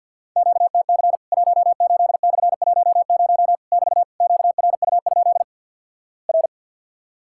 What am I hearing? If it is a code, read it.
Morse code, heard as a keyed tone, about 50 wpm, 700 Hz. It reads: QTC 18/10 <BT> QRU? R
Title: Worked All Europe DX Contest - QTC Exchange with Repeats - 50wpm